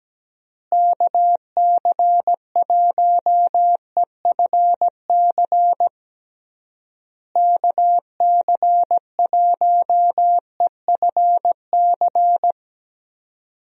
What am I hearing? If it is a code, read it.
Morse code, heard as a keyed tone, about 17 wpm, 700 Hz. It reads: KC1EFC KC1EFC
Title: USA Call Signs - 6 Characters - Repeated 3x - 17wpm